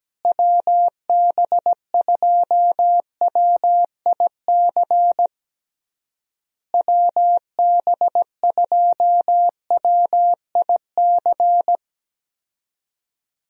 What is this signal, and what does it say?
Morse code, heard as a keyed tone, about 17 wpm, 700 Hz. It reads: WB2WIC WB2WIC
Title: USA Call Signs - 6 Characters - Repeated 3x - 17wpm